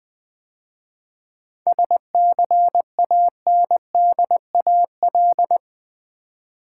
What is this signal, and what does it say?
Morse code, heard as a keyed tone, about 20 wpm, 700 Hz. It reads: SCANDAL
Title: 7 Letter Words - First 3 Letters Given - 20wpm